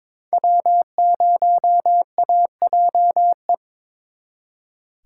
Morse code, heard as a keyed tone, about 26 wpm, 700 Hz. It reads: W0AJE
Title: Speed Racing - USA 1x3 Call Signs 22wpm